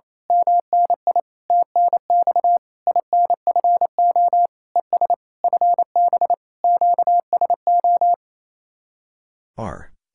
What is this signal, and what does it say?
Morse code, heard as a keyed tone, about 26 wpm, 700 Hz. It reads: MNI TNX INFO ES FB QSO
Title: Speed-Racing Common QSO Phrases - 28wpm